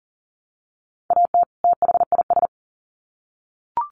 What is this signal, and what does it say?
Morse code, heard as a keyed tone, about 40 wpm, 700 Hz. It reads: AT THIS
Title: Sets of 2 Words 40wpm